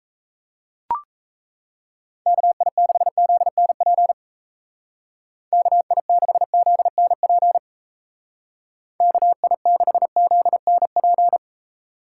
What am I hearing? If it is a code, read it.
Morse code, heard as a keyed tone, about 34 wpm, 700 Hz. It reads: KI6ZNP KI6ZNP KI6ZNP
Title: Speed Racing - USA 2x3 Call Signs 28wpm